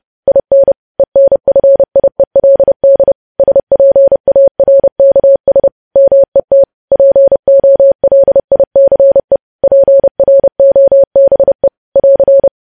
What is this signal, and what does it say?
Morse code, heard as a keyed tone, about 30 wpm, 550 Hz. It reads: IN ENFIELD SPARKS MET POLICE PROBE <AR>
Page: news headlines in morse code at 30 wpm